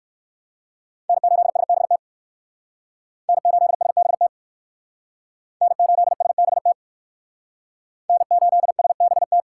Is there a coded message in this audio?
Yes: Morse code from 1.1 to 9.4 s, heard as a keyed tone, about 56 wpm, 700 Hz. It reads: N8SBT N8SBT N8SBT N8SBT